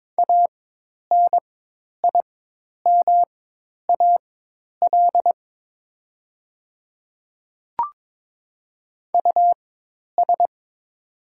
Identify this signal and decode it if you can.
Morse code, heard as a keyed tone, about 22 wpm, spaced as for 10 wpm, 700 Hz. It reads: ANIMAL US
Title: 4x Character Spacing - Top 300 Words - 22wpm